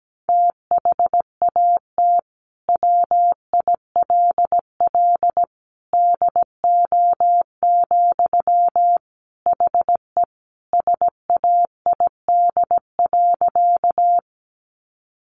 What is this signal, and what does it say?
Morse code, heard as a keyed tone, about 17 wpm, 700 Hz. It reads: THAT WILL DO, HE SAID.
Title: Sentences from Top 100 Words 17wpm